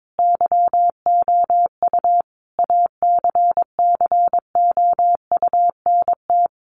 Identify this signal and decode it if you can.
Morse code, heard as a keyed tone, about 22 wpm, 700 Hz. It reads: YOU ACCOUNT
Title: Sentences from Top 900 Words 22wpm